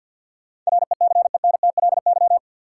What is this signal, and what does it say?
Morse code, heard as a keyed tone, about 47 wpm, 700 Hz. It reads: RECENTLY
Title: Sentences from Top 1200 Words - New Word Drill - 50wpm